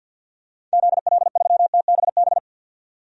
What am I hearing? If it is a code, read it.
Morse code, heard as a keyed tone, about 47 wpm, 700 Hz. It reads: ZL3TBB
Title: DXCC-100 Call Signs - 6 Characters - 50wpm